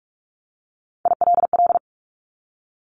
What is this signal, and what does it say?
Morse code, heard as a keyed tone, about 44 wpm, 700 Hz. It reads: ILL
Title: Top 1751-2000 Words 45wpm